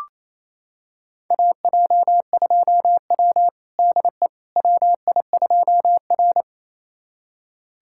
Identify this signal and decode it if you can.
Morse code, heard as a keyed tone, about 28 wpm, 700 Hz. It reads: AJ2W DE WI2R
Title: USA Callsign DE Callsign - 4 Characters - 28wpm